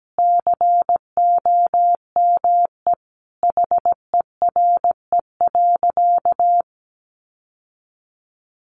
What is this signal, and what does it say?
Morse code, heard as a keyed tone, about 17 wpm, 700 Hz. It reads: COME HERE.
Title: Sentences from Top 1300 Words 17wpm